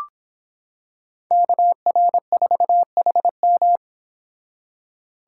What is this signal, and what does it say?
Morse code, heard as a keyed tone, about 25 wpm, 700 Hz. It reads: KR4HM